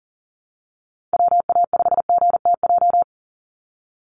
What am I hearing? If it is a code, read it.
Morse code, heard as a keyed tone, about 40 wpm, 700 Hz. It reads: WA5GTJ
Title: USA Call Signs 40wpm